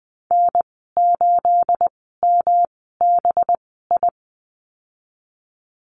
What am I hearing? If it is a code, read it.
Morse code, heard as a keyed tone, about 20 wpm, spaced as for 14 wpm, 700 Hz. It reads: N8MBI
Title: 231 Lesson 37g - Letter x - Call Signs 20wpm